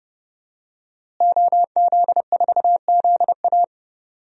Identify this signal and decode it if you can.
Morse code, heard as a keyed tone, about 30 wpm, 700 Hz. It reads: OZ4ZA
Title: DXCC-100 Call Signs - 5 Characters - Repeated 2x - 30wpm